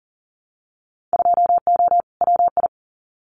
Morse code, heard as a keyed tone, about 40 wpm, 700 Hz. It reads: 2O WI
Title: Winter Field Day Exchanges 40wpm